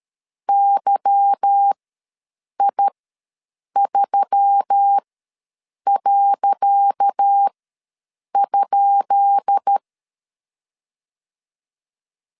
Morse code, heard as a keyed tone, about 13 wpm, spaced as for 7 wpm, 800 Hz. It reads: YI3.?
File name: LEZIONI / 10 SIGNS / 10C DOT ws 12wpm 3548